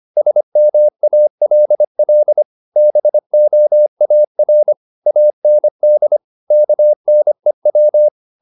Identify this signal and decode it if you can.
Morse code, heard as a keyed tone, about 25 wpm, 600 Hz. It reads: SMALL BOAR AND KNEW
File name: the old man and the sea part 1 25wpm